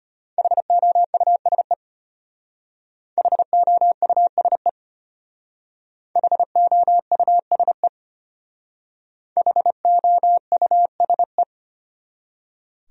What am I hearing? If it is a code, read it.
Morse code, heard as a keyed tone, about 31 wpm, 700 Hz. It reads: HOUSE HOUSE HOUSE HOUSE
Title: Speed-Racing Top 200 Words 25wpm